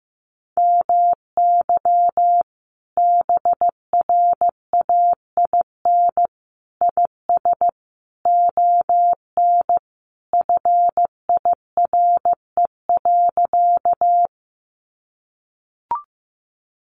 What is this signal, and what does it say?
Morse code, heard as a keyed tone, about 15 wpm, 700 Hz. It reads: MY BRAIN IS ON FIRE.
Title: Sentences from Top 1100 Words 15wpm